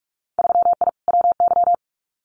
Morse code, heard as a keyed tone, about 46 wpm, 700 Hz. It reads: 3I WY
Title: Winter Field Day Exchanges Encore 45wpm